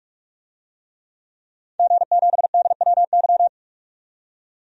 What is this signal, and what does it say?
Morse code, heard as a keyed tone, about 46 wpm, 700 Hz. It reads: G7DWY
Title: DXCC-100 Call Signs - 5 Characters - 45wpm